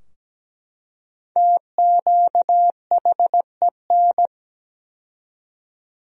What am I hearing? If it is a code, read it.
Morse code, heard as a keyed tone, about 17 wpm, 700 Hz. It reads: TQHEN
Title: 4x Word-Spacing - Sentences from Top 100 Words - 17wpm